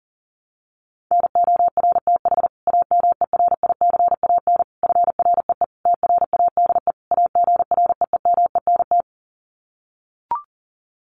Morse code, heard as a keyed tone, about 40 wpm, 700 Hz. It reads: NORTH AMERICAN FREE TRADE AGREEMENT
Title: Sets of 5 Words Encore 40wpm